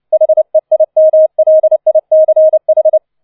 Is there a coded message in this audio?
Yes: Morse code from 0.1 to 3.0 s, heard as a keyed tone, about 29 wpm, 630 Hz. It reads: HEIMLICH